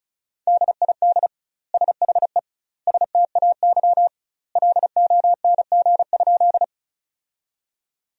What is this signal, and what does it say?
Morse code, heard as a keyed tone, about 34 wpm, 700 Hz. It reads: DID SHE STAY LONG?
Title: 2x Word-Spacing - Sentences from Top 500 Words - 35wpm